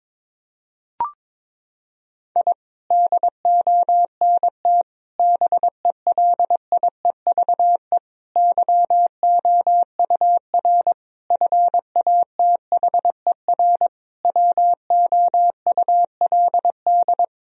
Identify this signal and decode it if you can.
Morse code, heard as a keyed tone, about 22 wpm, 700 Hz. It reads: I DONT BELIEVE YOUR FATHER WOULD
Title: Sentences from Top 1000 Words - Review Part 3 - 22wpm